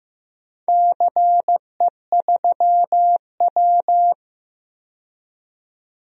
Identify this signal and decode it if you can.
Morse code, heard as a keyed tone, about 15 wpm, 700 Hz. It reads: CE3W
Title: DXCC-100 Call Signs - 4 Characters - 15wpm